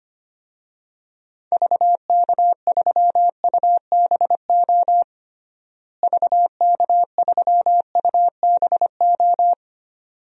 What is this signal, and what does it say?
Morse code, heard as a keyed tone, about 25 wpm, 700 Hz. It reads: VK3UBO VK3UBO
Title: DXCC-100 Call Signs - 6 Characters - Repeated 2x - 25wpm